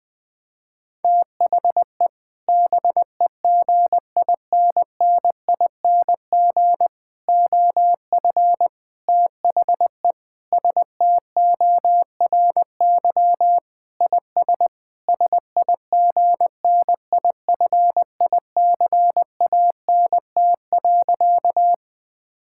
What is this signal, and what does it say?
Morse code, heard as a keyed tone, about 21 wpm, 700 Hz. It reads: THE BEGINNING OF THE STORY IS SIGNIFICANT.
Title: Sentences from Top 1000 Words - Review Part 6 - 20wpm